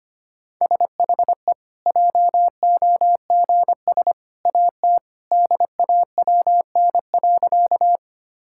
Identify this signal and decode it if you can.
Morse code, heard as a keyed tone, about 25 wpm, 700 Hz. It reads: SHE JOGS AT DAWN.
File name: Easy Sentences with Hints - 25wpm